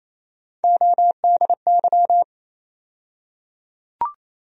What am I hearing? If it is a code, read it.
Morse code, heard as a keyed tone, about 27 wpm, 700 Hz. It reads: ODY
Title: USA Call Sign Postfixes - 3 Letters 28wpm